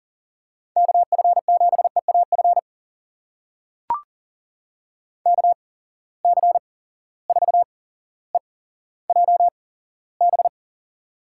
Morse code, heard as a keyed tone, about 40 wpm, spaced as for 17 wpm, 700 Hz. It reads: KF7EAF KC4EJB
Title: USA Call Signs - 6 Characters - 8x Character Spacing - 40wpm